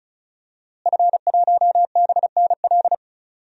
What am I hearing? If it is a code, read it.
Morse code, heard as a keyed tone, about 36 wpm, 700 Hz. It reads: F1BNL
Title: DXCC-100 Call Signs - 5 Characters - Repeated 2x - 35wpm